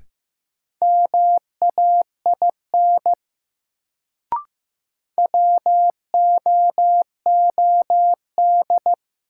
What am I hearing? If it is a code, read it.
Morse code, heard as a keyed tone, about 16 wpm, 700 Hz. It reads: MAIN WOOD
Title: Sentences from Top 1000 Words - Review Part 2 - New Word Drill - 15wpm